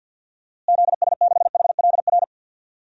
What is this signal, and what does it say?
Morse code, heard as a keyed tone, about 52 wpm, 700 Hz. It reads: ZS6HLR
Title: DXCC-100 Call Signs - 6 Characters - Repeated 3x - 50wpm